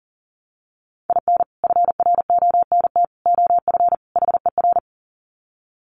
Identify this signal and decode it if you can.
Morse code, heard as a keyed tone, about 37 wpm, 700 Hz. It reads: IN FRONT OF HER
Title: Sets of 4 Words 40wpm